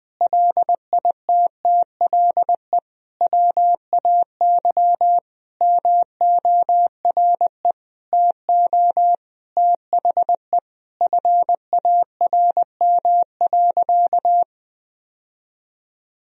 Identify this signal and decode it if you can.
Morse code, heard as a keyed tone, about 21 wpm, 700 Hz. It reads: LITTLE WAY MORE TO THE FARM.
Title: Sentences from Top 1000 Words - Review Part 2 - 20wpm